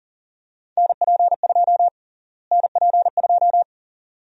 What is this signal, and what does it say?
Morse code, heard as a keyed tone, about 40 wpm, 700 Hz. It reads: NP2 NP2
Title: USA Call Sign Prefixes - Repeated 2x - 40wpm